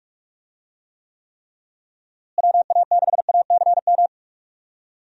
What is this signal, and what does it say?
Morse code, heard as a keyed tone, about 41 wpm, 700 Hz. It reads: WA6ACM
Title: USA Call Signs - 6 Characters - Repeated 3x - 45wpm